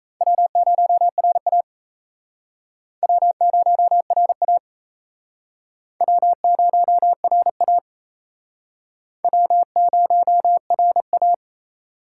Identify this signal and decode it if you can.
Morse code, heard as a keyed tone, about 34 wpm, 700 Hz. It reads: W0RA W0RA W0RA W0RA
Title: Speed Racing - USA Call Signs 28wpm